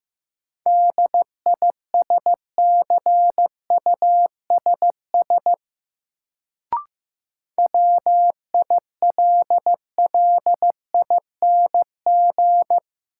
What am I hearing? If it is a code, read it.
Morse code, heard as a keyed tone, about 15 wpm, 700 Hz. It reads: DISCUSS WILLING
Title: Sentences from Top 1000 Words - New Word Drill - 15wpm